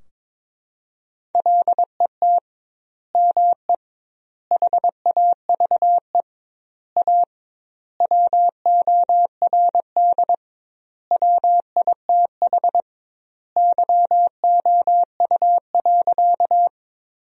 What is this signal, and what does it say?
Morse code, heard as a keyed tone, about 22 wpm, 700 Hz. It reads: LET ME HAVE A WORD WITH YOU.
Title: 2x Word-Spacing - Sentences from Top 300 Words - 22wpm